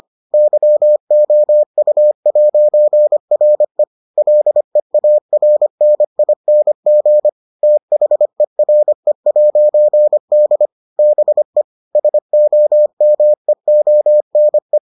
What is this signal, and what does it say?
Morse code, heard as a keyed tone, about 25 wpm, 600 Hz. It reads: YOU'RE LEARNING THERE'D BE SOMEONE